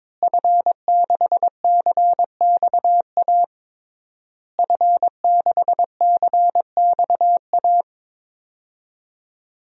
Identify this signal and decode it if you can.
Morse code, heard as a keyed tone, about 22 wpm, 700 Hz. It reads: F6CXA F6CXA
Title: DXCC-100 Call Signs - 5 Characters - Repeated 3x - 22wpm